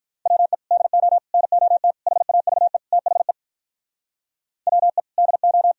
Morse code, heard as a keyed tone, about 44 wpm, 700 Hz. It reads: WE DO NOT HAVE THE WE DO